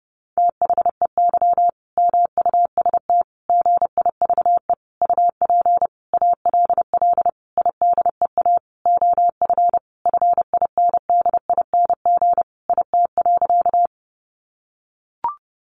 Morse code, heard as a keyed tone, about 30 wpm, 700 Hz. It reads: THEY MUST GIVE UP ALL IDEA OF FINDING IT.
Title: Sentences from Top 1100 Words 30wpm